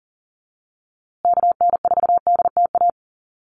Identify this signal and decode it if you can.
Morse code, heard as a keyed tone, about 40 wpm, 700 Hz. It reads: KN4DTA
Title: USA Call Signs Repeated 2x 40wpm